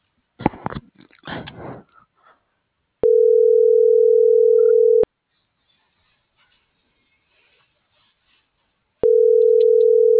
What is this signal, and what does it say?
Ringback tone (North America), 440 + 480 Hz.